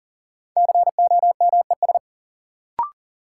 Morse code, heard as a keyed tone, about 40 wpm, 700 Hz. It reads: COMES
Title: Sentences from Top 1000 Words - Review Part 5 - New Word Drill - 40wpm